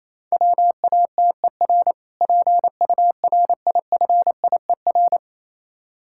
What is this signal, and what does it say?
Morse code, heard as a keyed tone, about 27 wpm, 700 Hz. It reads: WATER PURIFIER
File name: Easy Words - Speed-Racing with Hints - 22wpm